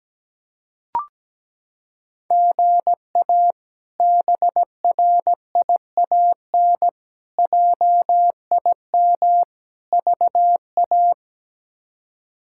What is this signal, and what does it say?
Morse code, heard as a keyed tone, about 17 wpm, 700 Hz. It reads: GA BRIAN JIM VA
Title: K1USN SST QSO Practice - 17wpm